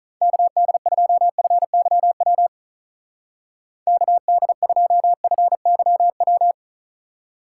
Morse code, heard as a keyed tone, about 44 wpm, 700 Hz. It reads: KD2FYW KD2FYW